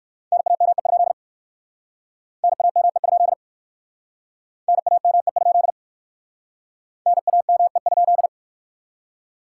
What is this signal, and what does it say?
Morse code, heard as a keyed tone, about 53 wpm, 700 Hz. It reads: NAME? NAME? NAME? NAME?
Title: Speed-Racing Common QSO Phrases Encore - 45wpm